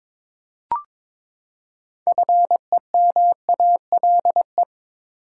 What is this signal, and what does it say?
Morse code, heard as a keyed tone, about 22 wpm, 700 Hz. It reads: FEMALE
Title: Six Letter Words - 22wpm